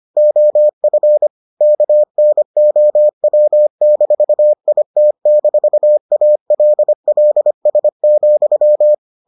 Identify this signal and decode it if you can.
Morse code, heard as a keyed tone, about 25 wpm, 600 Hz. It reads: OF KNOW-IT-ALLS,